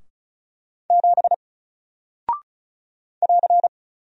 Morse code, heard as a keyed tone, about 36 wpm, 700 Hz. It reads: 7 <AR>
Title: Single Character (Letter:Number:Punctuation:Prosign) 35wpm